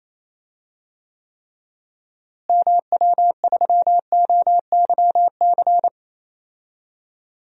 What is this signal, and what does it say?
Morse code, heard as a keyed tone, about 28 wpm, 700 Hz. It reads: MW3OYC
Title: DXCC-100 Call Signs - 6 Characters - Repeated 2x - 28wpm